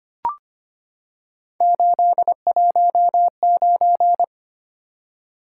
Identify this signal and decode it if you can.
Morse code, heard as a keyed tone, about 25 wpm, 700 Hz. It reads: 819
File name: Three Letter-Number Combinations - 25wpm